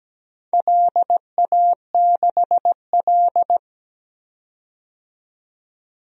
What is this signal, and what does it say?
Morse code, heard as a keyed tone, about 17 wpm, 700 Hz. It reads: LA6L
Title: DXCC-100 Call Signs - 4 Characters - 17wpm